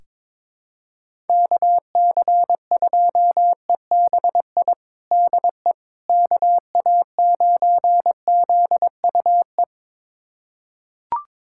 Morse code, heard as a keyed tone, about 22 wpm, 700 Hz. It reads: KC2EBI DE KA9ZUE
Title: USA Callsign DE Callsign 22wpm